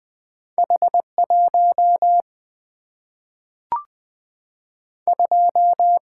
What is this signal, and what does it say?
Morse code, heard as a keyed tone, about 20 wpm, 700 Hz. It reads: H 1 2
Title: Single-Letter Single-Number 20wpm